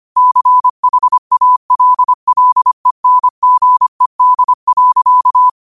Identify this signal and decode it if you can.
Morse code, heard as a keyed tone, about 25 wpm, 1000 Hz. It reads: CHALLENGED.
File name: qotdfiles / 25wpm